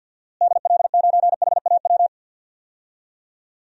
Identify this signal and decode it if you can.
Morse code, heard as a keyed tone, about 49 wpm, 700 Hz. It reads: DL9HAW
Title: DXCC-100 Call Signs - 6 Characters - Repeated 3x - 50wpm